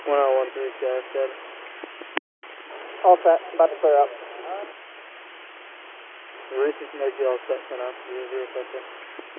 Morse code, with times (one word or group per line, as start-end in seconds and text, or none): none